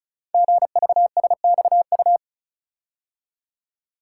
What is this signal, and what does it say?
Morse code, heard as a keyed tone, about 36 wpm, 700 Hz. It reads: GVSXU
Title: Code Groups - 5 Letters - 35wpm